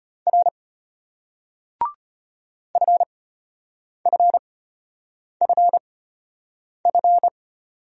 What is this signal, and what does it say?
Morse code, heard as a keyed tone, about 32 wpm, spaced as for 8 wpm, 700 Hz. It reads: R FFFF